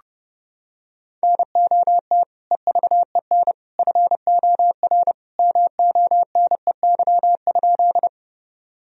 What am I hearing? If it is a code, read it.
Morse code, heard as a keyed tone, about 32 wpm, 700 Hz. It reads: NOT EVEN FOR MONEY?
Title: Sentences from Top 1000 Words - Review Part 1 - 30wpm